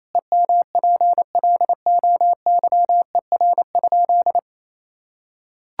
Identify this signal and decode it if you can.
Morse code, heard as a keyed tone, about 28 wpm, 700 Hz. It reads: EMPLOYER?
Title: Sentences from Top 1600 Words 28wpm